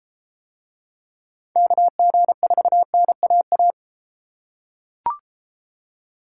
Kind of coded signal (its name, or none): Morse code